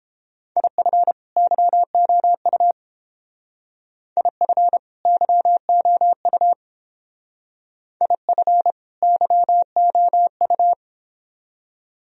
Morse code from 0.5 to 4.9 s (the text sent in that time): IF YOU IF